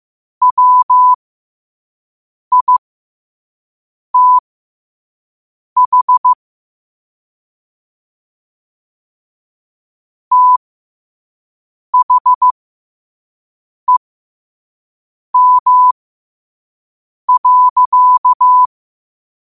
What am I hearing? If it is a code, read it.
Morse code, heard as a keyed tone, about 15 wpm, spaced as for 5 wpm, 1000 Hz. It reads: WITH THEM.
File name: qotdfiles / 5wpm